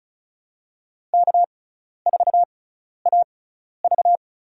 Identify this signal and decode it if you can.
Morse code, heard as a keyed tone, about 36 wpm, spaced as for 12 wpm, 700 Hz. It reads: K4AV
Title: USA Call Signs - 6x Character Spacing - 35wpm